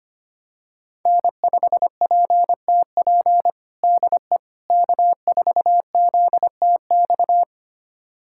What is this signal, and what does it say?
Morse code, heard as a keyed tone, about 25 wpm, 700 Hz. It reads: N5PTP DE K4ZTX
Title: USA Callsign DE Callsign - 5 Characters - 25wpm